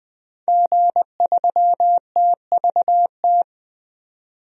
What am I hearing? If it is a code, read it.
Morse code, heard as a keyed tone, about 20 wpm, 700 Hz. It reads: G3TVT
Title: DXCC-100 Call Signs - 5 Characters - 20wpm